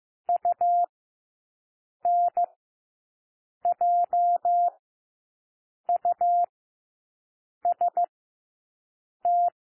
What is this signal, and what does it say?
Morse code, heard as a keyed tone, about 15 wpm, spaced as for 6 wpm, 700 Hz. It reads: UNJUST
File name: cwmp3 / W1 SOUND WORDS 15 wpm